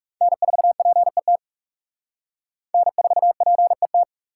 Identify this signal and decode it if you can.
Morse code, heard as a keyed tone, about 37 wpm, 700 Hz. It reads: N4PET N4PET
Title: Speed Racing - USA 1x3 Call Signs 30wpm